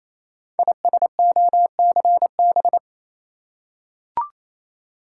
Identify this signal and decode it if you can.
Morse code, heard as a keyed tone, about 27 wpm, 700 Hz. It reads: ISOCB
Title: Code Groups - 5 Letters - 28wpm